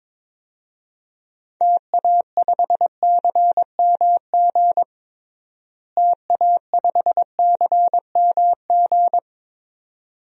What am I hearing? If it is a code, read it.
Morse code, heard as a keyed tone, about 22 wpm, 700 Hz. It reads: TA5CMG TA5CMG
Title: DXCC-100 Call Signs - 6 Characters - Repeated 2x - 22wpm